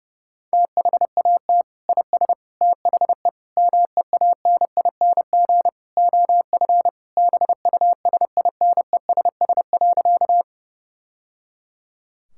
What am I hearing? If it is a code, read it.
Morse code, heard as a keyed tone, about 28 wpm, 700 Hz. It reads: THAT IS THE MEANING OF BUSINESS.
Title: Sentences from Top 1000 Words - Review Part 6 - 30wpm